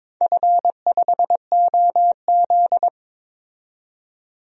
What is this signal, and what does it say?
Morse code, heard as a keyed tone, about 22 wpm, 700 Hz. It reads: F5OZ
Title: DXCC-100 Call Signs - 4 Characters - 22wpm